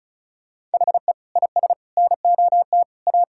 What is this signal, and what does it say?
Morse code, heard as a keyed tone, about 36 wpm, 700 Hz. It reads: HE IS NOT A
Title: Sentences from Top 1600 Words 35wpm